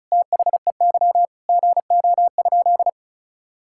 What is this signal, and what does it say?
Morse code, heard as a keyed tone, about 34 wpm, 700 Hz. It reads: THEY GO?
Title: Sentences from Top 1600 Words 35wpm